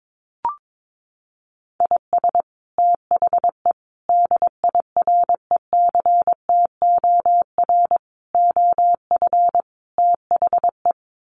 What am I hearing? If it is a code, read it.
Morse code, heard as a keyed tone, about 22 wpm, 700 Hz. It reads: IS THE DIRECTOR OF THE
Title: Sets of 5 Words Encore2 22wpm